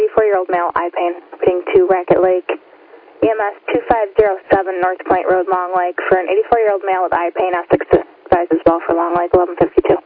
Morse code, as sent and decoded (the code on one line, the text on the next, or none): none